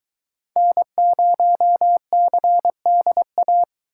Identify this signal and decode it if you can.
Morse code, heard as a keyed tone, about 22 wpm, 700 Hz. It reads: N0CDA